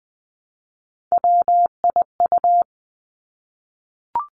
Morse code, heard as a keyed tone, about 20 wpm, 700 Hz. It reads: WIU
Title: Three Challenging Letter-Number Combinations 20wpm